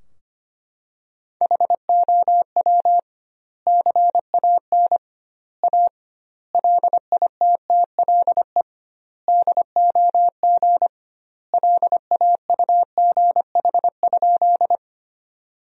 Morse code, heard as a keyed tone, about 25 wpm, 700 Hz. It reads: HOW CAN A LITTLE DOG LAUGH?
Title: 2x Word-Spacing - Sentences from Top 500 Words - 25wpm